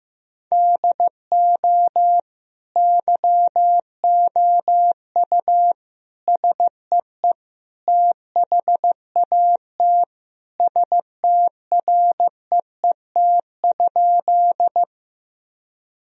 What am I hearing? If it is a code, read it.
Morse code, heard as a keyed tone, about 15 wpm, 700 Hz. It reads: DO YOU SEE THAT STREET?
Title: Sentences from Top 1000 Words - Review Part 3 - 15wpm